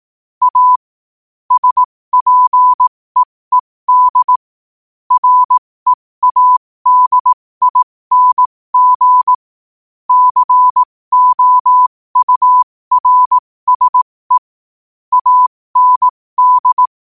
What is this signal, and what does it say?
Morse code, heard as a keyed tone, about 18 wpm, spaced as for 15 wpm, 1000 Hz. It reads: A SPEED READING COURSE AND
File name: qotdfiles / 15wpm